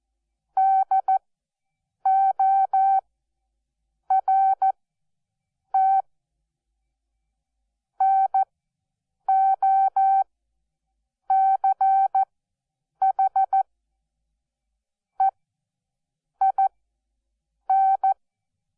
Morse code, heard as a keyed tone, about 14 wpm, spaced as for 7 wpm, 770 Hz. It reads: DORT NOCH EIN
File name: Onlinelehrgang / morsen / DJ4UF Morsen12